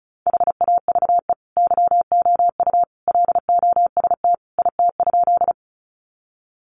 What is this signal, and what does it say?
Morse code, heard as a keyed tone, about 34 wpm, 700 Hz. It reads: HAVE YOU LOST IT?
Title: Sentences from Top 1400 Words 35wpm